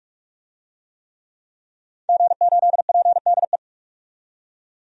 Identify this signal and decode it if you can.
Morse code, heard as a keyed tone, about 43 wpm, 700 Hz. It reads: G8PDE
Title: DXCC-100 Call Signs - 5 Characters - 45wpm